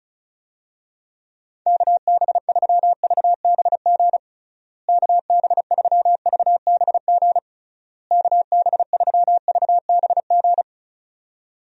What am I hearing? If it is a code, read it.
Morse code, heard as a keyed tone, about 36 wpm, 700 Hz. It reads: KB3VBG KB3VBG KB3VBG
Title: USA Call Signs - 6 Characters - Repeated 3x - 35wpm